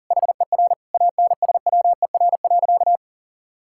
Morse code, heard as a keyed tone, about 40 wpm, 700 Hz. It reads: HER ANSWER.
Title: Sentences from Top 1600 Words 40wpm